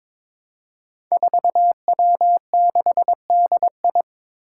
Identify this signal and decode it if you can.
Morse code, heard as a keyed tone, about 22 wpm, 700 Hz. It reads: 4W6DI
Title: Five Challenging Letter-Number Combinations 22wpm